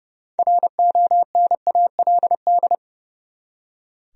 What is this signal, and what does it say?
Morse code, heard as a keyed tone, about 32 wpm, 700 Hz. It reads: RONALD
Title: USA Common Names - 30wpm